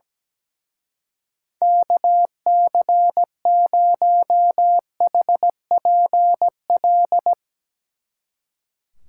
Morse code, heard as a keyed tone, about 18 wpm, 700 Hz. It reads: KC0HPL
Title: Speed Racing - USA 2x3 Call Signs 17wpm